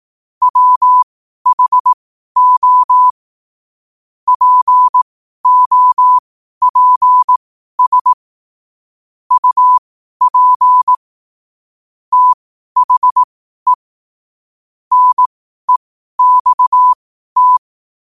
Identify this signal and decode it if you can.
Morse code, heard as a keyed tone, about 18 wpm, spaced as for 12 wpm, 1000 Hz. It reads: WHO POPS UP THE NEXT